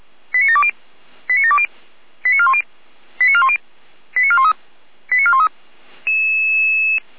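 Five-tone selective call (ZVEI), heard as quick runs of five tones, tones of about 80 ms. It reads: 89720 89720 89410 89410 89413 89413
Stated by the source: Five-tone ZVEI selective call 89401 received via SDR.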